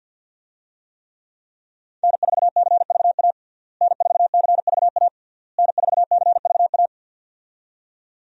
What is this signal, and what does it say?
Morse code, heard as a keyed tone, about 51 wpm, 700 Hz. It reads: N4CVA N4CVA N4CVA
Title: USA Call Signs - 5 Characters - Repeated 3x - 50wpm